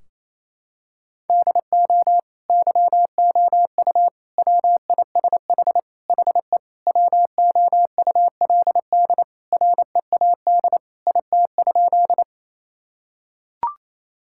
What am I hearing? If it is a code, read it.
Morse code, heard as a keyed tone, about 27 wpm, 700 Hz. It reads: DO YOU WISH HE WOULD READ IT?
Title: Sentences from Top 1600 Words 28wpm